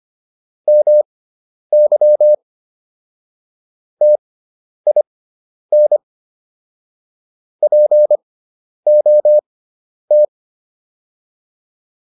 Morse code, heard as a keyed tone, about 25 wpm, spaced as for 10 wpm, 600 Hz. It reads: MY TIN POT